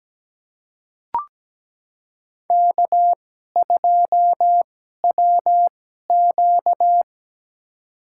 Morse code, heard as a keyed tone, about 17 wpm, spaced as for 12 wpm, 700 Hz. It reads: K2WQ
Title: USA Call Signs - 2x Character Spacing - 17wpm